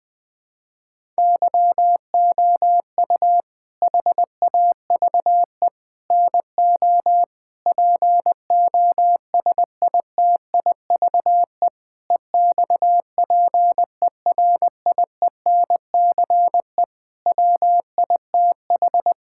Morse code, heard as a keyed tone, about 21 wpm, 700 Hz. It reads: YOU HAVE NO POSITIVE EXPERIENCE WITH
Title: Sentences from Top 1000 Words - Review Part 5 - 20wpm